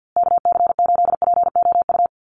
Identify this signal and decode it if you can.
Morse code, heard as a keyed tone, about 50 wpm, 700 Hz. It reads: KC8POU
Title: USA Call Signs Repeated 3x 50wpm